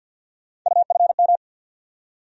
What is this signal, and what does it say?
Morse code, heard as a keyed tone, about 50 wpm, 700 Hz. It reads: UFM
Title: USA Call Sign Postfixes - 3 Letters 50wpm